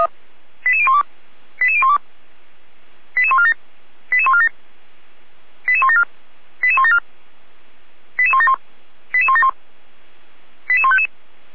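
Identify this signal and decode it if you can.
Five-tone selective call (ZVEI), heard as quick runs of five tones, tones of about 80 ms. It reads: 88012 88012 88147 88147 88174 88174 88171 88171 88155
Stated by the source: ZVEI five-tone call received by SDR.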